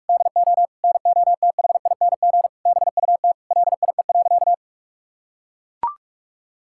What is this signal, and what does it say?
Morse code, heard as a keyed tone, about 44 wpm, 700 Hz. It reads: DO NOTHING BUT LIE.
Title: Sentences from Top 1600 Words 45wpm